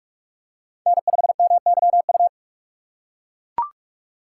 Morse code, heard as a keyed tone, about 41 wpm, 700 Hz. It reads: N5MYU
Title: USA Call Signs - 5 Characters - 3x Character Spacing - 45wpm